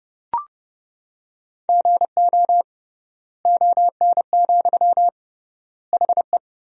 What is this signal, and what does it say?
Morse code, heard as a keyed tone, about 30 wpm, 700 Hz. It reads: GO ON, HE
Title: 3x Word-Spacing - Sentences from Top 100 Words - 30wpm